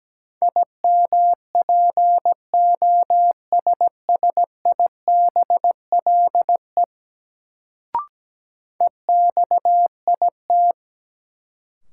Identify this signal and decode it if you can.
Morse code, heard as a keyed tone, about 17 wpm, 700 Hz. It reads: IMPOSSIBLE EXIT
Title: Sentences from Top 1100 Words - New Word Drill Alt - 17wpm